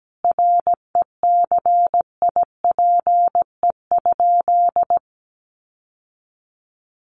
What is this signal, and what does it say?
Morse code, heard as a keyed tone, about 17 wpm, 700 Hz. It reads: RECIPE?
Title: Sentences from Top 1200 Words 17wpm